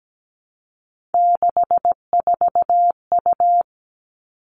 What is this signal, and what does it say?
Morse code, heard as a keyed tone, about 17 wpm, 700 Hz. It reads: 64U
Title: Three Challenging Letter-Number Combinations 17wpm